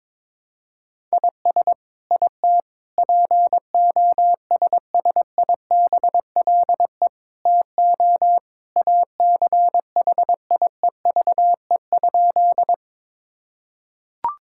Morse code, heard as a keyed tone, about 22 wpm, 700 Hz. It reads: IS IT POSSIBLE TO ACHIEVE?
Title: Sentences from Top 1000 Words - Review Part 6 - 22wpm